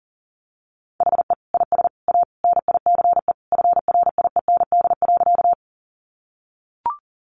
Morse code, heard as a keyed tone, about 40 wpm, 700 Hz. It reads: HE IS A NICE FRIEND.
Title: Sentences from Top 1100 Words 40wpm